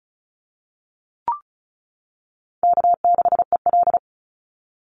Morse code, heard as a keyed tone, about 36 wpm, 700 Hz. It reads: K6EL
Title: NA SOTA Chaser Callsigns 35wpm